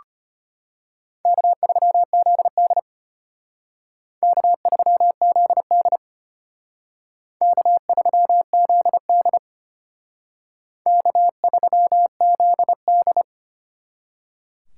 Morse code, heard as a keyed tone, about 32 wpm, 700 Hz. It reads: K3ZD K3ZD K3ZD K3ZD